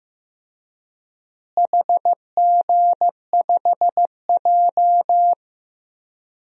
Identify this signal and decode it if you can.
Morse code, heard as a keyed tone, about 15 wpm, 700 Hz. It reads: HG5J
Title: DXCC-100 Call Signs - 4 Characters - Repeated 3x - 15wpm